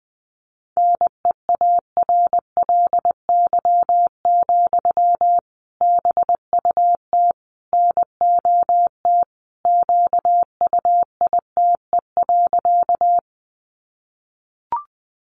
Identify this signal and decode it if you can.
Morse code, heard as a keyed tone, about 20 wpm, 700 Hz. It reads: NEARLY, BUT NOT QUITE.
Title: Sentences from Top 1300 Words 20wpm